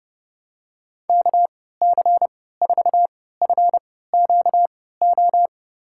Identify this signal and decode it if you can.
Morse code, heard as a keyed tone, about 30 wpm, spaced as for 17 wpm, 700 Hz. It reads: KC4FQO
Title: USA Call Signs - 3x Character Spacing - 30wpm